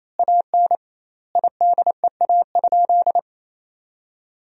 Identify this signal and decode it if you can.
Morse code, heard as a keyed tone, about 28 wpm, 700 Hz. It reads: AN IDEA?
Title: 2x Word-Spacing - Sentences from Top 400 Words - 28wpm